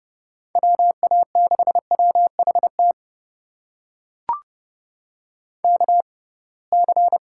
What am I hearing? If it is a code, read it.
Morse code, heard as a keyed tone, about 32 wpm, spaced as for 23 wpm, 700 Hz. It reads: WA6WHT KC
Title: USA Call Signs - 6 Characters - 6x Character Spacing - 30wpm